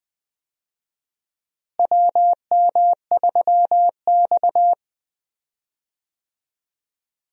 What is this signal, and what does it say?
Morse code, heard as a keyed tone, about 20 wpm, 700 Hz. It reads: WM3X
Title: USA Call Signs - 4 Characters - Repeated 3x - 20wpm